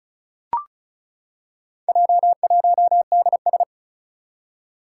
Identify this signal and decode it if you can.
Morse code, heard as a keyed tone, about 36 wpm, 690 Hz. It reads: J1DS
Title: Four Challenging Letter-Number Combinations 35wpm